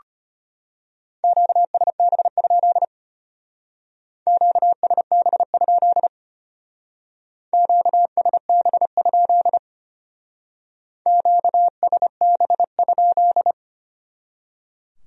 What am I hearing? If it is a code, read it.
Morse code, heard as a keyed tone, about 30 wpm, 700 Hz. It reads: QSB? QSB? QSB? QSB?
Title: Speed-Racing Q-Codes - 25wpm